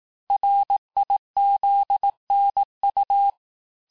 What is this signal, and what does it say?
Morse code, heard as a keyed tone, about 18 wpm, 800 Hz. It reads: RIZNU
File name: cours-du-lundi files / cours-1594--teste-chiffre-vitesse-17mpm